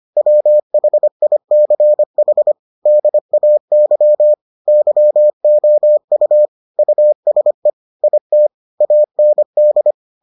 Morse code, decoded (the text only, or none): WHICH DAY YOU USE IT AND